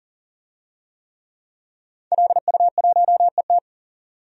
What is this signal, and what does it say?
Morse code, heard as a keyed tone, about 40 wpm, 700 Hz. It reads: LU1ET
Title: DXCC-100 Call Signs - 5 Characters - 40wpm